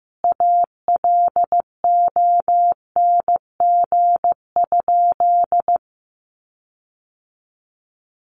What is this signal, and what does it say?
Morse code, heard as a keyed tone, about 15 wpm, 700 Hz. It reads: ALONG?
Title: Sentences from Top 1300 Words 15wpm